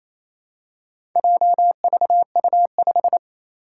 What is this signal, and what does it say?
Morse code, heard as a keyed tone, about 28 wpm, 700 Hz. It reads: JVU5
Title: Four Challenging Letter-Number Combinations 28wpm